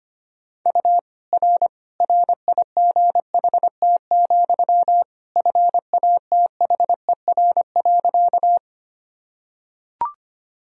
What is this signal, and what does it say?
Morse code, heard as a keyed tone, about 25 wpm, 700 Hz. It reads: U R RIGHT, FATHER.
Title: Sentences from Top 300 Words and Abbreviations 25wpm